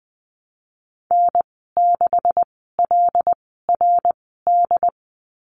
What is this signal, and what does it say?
Morse code, heard as a keyed tone, about 20 wpm, spaced as for 15 wpm, 700 Hz. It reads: N6LRD
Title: 193 Lesson 31g - Letter v - Call Signs 20wpm